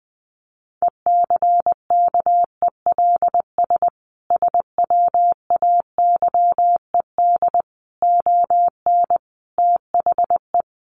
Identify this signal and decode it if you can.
Morse code, heard as a keyed tone, about 20 wpm, 700 Hz. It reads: ECKELS SWAYED ON THE
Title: A Sound Of Thunder 20wpm